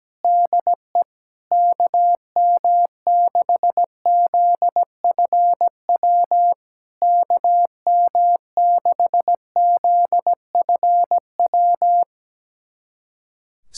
Morse code, heard as a keyed tone, about 17 wpm, 700 Hz. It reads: DE KM6ZFW KM6ZFW
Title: SOTA QSO Practice 17wpm